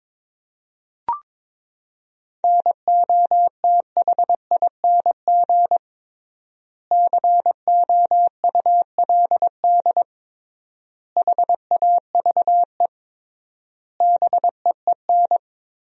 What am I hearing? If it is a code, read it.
Morse code, heard as a keyed tone, about 22 wpm, 700 Hz. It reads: NOTHING COULD HAVE BEEN
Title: 3x Word-Spacing - Sentences from Top 500 Words - 22wpm